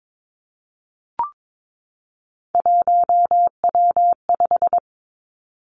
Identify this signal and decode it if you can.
Morse code, heard as a keyed tone, about 22 wpm, 700 Hz. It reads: 1W5
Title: Three Challenging Letter-Number Combinations 22wpm